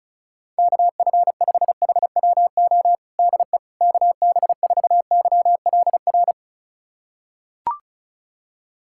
Morse code, heard as a keyed tone, about 34 wpm, 700 Hz. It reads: KF5HWO DE KB4YLR
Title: USA Callsign DE Callsign - 6 Characters - 35wpm